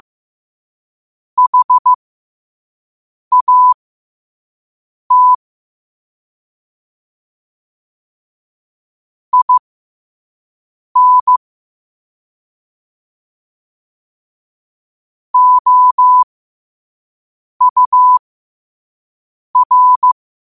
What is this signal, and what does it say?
Morse code, heard as a keyed tone, about 15 wpm, spaced as for 5 wpm, 1000 Hz. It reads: HAT IN OUR
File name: qotdfiles / 5wpm